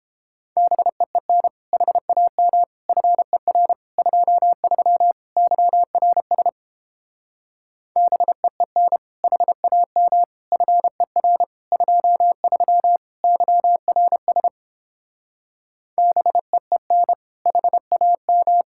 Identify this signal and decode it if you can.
Morse code, heard as a keyed tone, about 29 wpm, 700 Hz. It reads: BEEN HAM FER 23 YRS BEEN HAM FER 23 YRS BEEN HAM